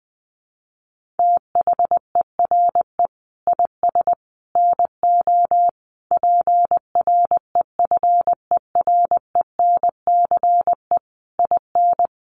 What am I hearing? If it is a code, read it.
Morse code, heard as a keyed tone, about 20 wpm, 700 Hz. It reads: THERE IS NO PREFERENCE IN
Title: Sentences from Top 1500 Words 20wpm